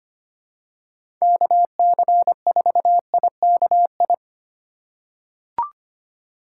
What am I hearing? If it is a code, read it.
Morse code, heard as a keyed tone, about 25 wpm, 700 Hz. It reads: KC4IKI
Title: USA Call Signs - 6 Characters - 2x Character Spacing - 25wpm